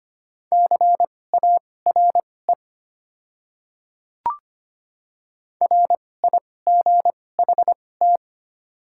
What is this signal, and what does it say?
Morse code, heard as a keyed tone, about 25 wpm, spaced as for 18 wpm, 700 Hz. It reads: CARE RIGHT
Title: 2x Character Spacing - Top 300 Words - 25wpm